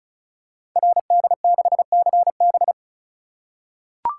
Morse code, heard as a keyed tone, about 36 wpm, 690 Hz. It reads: RD6CB
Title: Code Groups - 5 Letter:Numbers - 35wpm